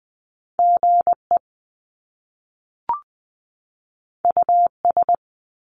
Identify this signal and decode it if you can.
Morse code, heard as a keyed tone, about 20 wpm, 700 Hz. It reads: GE US
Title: Top Two Letter Combinations in Words 20wpm